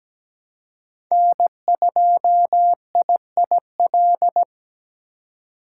Morse code, heard as a keyed tone, about 17 wpm, 700 Hz. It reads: N2IIL
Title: USA Call Signs - 5 Characters - 3x Character Spacing - 17wpm